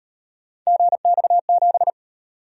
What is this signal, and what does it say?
Morse code, heard as a keyed tone, about 37 wpm, 700 Hz. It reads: GX7